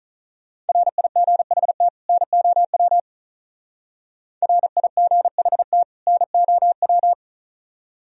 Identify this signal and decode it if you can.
Morse code, heard as a keyed tone, about 42 wpm, 700 Hz. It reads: RIGHT NOW RIGHT NOW
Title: Speed-Racing - Sets of 2 Words 35wpm